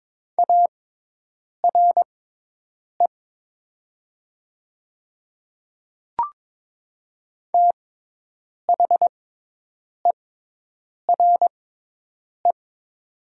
Morse code, heard as a keyed tone, about 22 wpm, spaced as for 8 wpm, 700 Hz. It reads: ARE THERE
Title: 6x Character Spacing - Top 200 Words - 22wpm